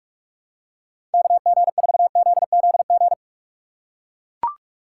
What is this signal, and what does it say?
Morse code, heard as a keyed tone, about 44 wpm, 700 Hz. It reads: KG4ZZG